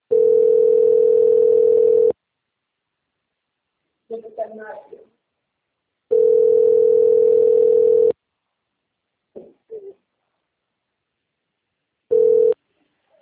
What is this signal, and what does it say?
Ringback tone (North America), 440 + 480 Hz, with speech.
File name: out- - - - - .46882